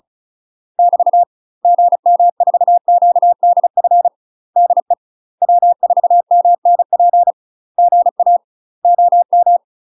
Morse code, heard as a keyed tone, about 36 wpm, 700 Hz. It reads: <BT> GM4QDF DE W4MNP GA OM
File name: UK MASTER QSOs TRAINING / 35WPM / UK MASTER QSOs 0012